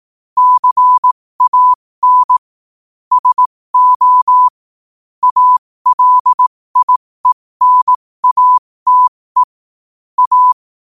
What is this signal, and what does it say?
Morse code, heard as a keyed tone, about 18 wpm, spaced as for 15 wpm, 1000 Hz. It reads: CAN SO ALIENATE A
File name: qotdfiles / 15wpm